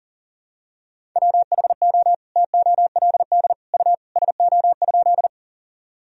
Morse code, heard as a keyed tone, about 40 wpm, 700 Hz. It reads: WHO TOLD U SO?
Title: Sentences from Top 400 Words and Abbreviations 40wpm